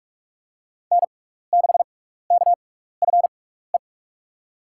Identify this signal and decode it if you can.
Morse code, heard as a keyed tone, about 44 wpm, spaced as for 15 wpm, 700 Hz. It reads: N6KFE
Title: USA Call Signs - 6x Character Spacing - 45wpm